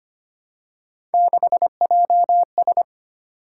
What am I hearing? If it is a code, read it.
Morse code, heard as a keyed tone, about 25 wpm, 700 Hz. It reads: 6JS
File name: Three Letter-Number Combinations - 25wpm